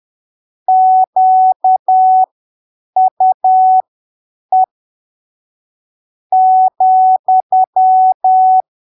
Morse code, heard as a keyed tone, about 10 wpm, spaced as for 7 wpm, 750 Hz. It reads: QUE ,
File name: arquivos / texto / texto 05 10ppm